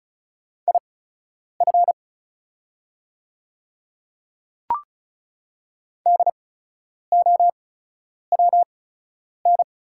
Morse code, heard as a keyed tone, about 36 wpm, spaced as for 10 wpm, 700 Hz. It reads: IF DOWN
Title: 8x Character Spacing - Top 100 Words - 35wpm